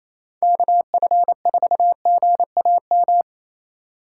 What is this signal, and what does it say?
Morse code, heard as a keyed tone, about 28 wpm, 700 Hz. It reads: KF4GAM
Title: USA Call Signs - 6 Characters - 3x Character Spacing - 28wpm